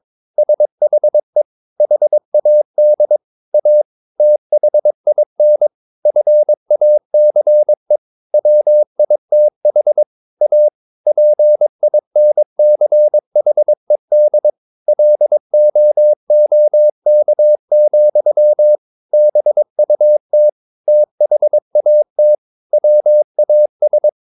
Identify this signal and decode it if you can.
Morse code, heard as a keyed tone, about 22 wpm, 600 Hz. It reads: SHE HAD A THIN FACE WITH A PINCHED LOOK, BUT THAT WAS
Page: Romanen Never af Ken Follet på engelsk.